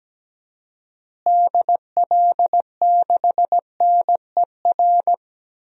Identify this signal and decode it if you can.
Morse code, heard as a keyed tone, about 17 wpm, 700 Hz. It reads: DL6NER
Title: DXCC-100 Call Signs - 6 Characters - Repeated 3x - 17wpm